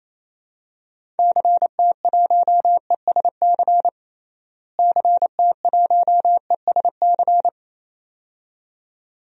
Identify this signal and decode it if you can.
Morse code, heard as a keyed tone, about 28 wpm, 700 Hz. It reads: CT1ESC CT1ESC
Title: DXCC-100 Call Signs - 6 Characters - Repeated 2x - 28wpm